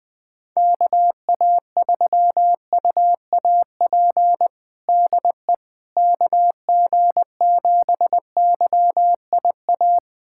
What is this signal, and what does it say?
Morse code, heard as a keyed tone, about 20 wpm, 700 Hz. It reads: KA3UAP DE KG7YIA
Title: USA Callsign DE Callsign - 6 Characters - 20wpm